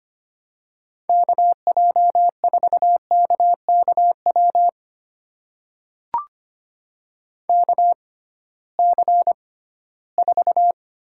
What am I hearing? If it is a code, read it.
Morse code, heard as a keyed tone, about 25 wpm, spaced as for 16 wpm, 700 Hz. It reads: KJ4KKW KC4
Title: USA Call Signs - 6 Characters - 6x Character Spacing - 25wpm